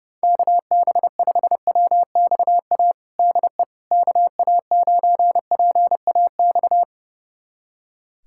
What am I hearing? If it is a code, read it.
Morse code, heard as a keyed tone, about 30 wpm, 700 Hz. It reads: KB5WXA DE KA9PAX
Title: USA Callsign DE Callsign - 6 Characters - 30wpm